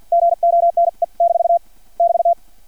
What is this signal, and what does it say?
Morse code, heard as a keyed tone, about 50 wpm, spaced as for 44 wpm, 680 Hz. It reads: GONE - <BT>
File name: archived / CRSnet